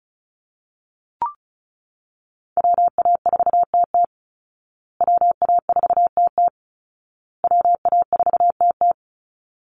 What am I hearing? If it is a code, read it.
Morse code, heard as a keyed tone, about 34 wpm, 700 Hz. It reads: WA4TT WA4TT WA4TT
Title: Call Signs Repeated 2x - 35wpm